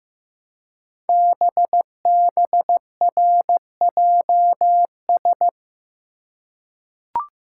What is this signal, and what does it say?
Morse code, heard as a keyed tone, about 15 wpm, 700 Hz. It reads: BBRJS
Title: Code Groups - 5 Letters - 15wpm